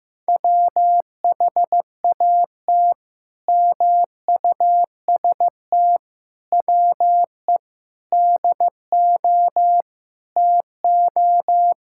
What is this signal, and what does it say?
Morse code, heard as a keyed tone, about 15 wpm, 700 Hz. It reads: WHAT MUST WE DO TO
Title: Sentences from Top 1000 Words - Review Part 6 - 15wpm